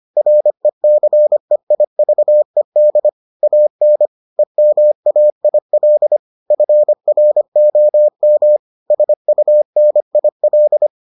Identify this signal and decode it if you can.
Morse code, heard as a keyed tone, about 25 wpm, 600 Hz. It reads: RECEIVED AN EMAIL FROM SUNIL